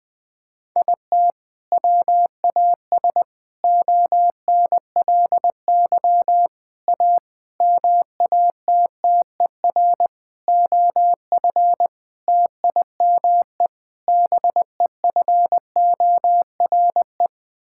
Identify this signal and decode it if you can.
Morse code, heard as a keyed tone, about 19 wpm, 700 Hz. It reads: IT WAS ONLY A MATTER OF TIME BEFORE
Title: Sets of 8 Words - 20wpm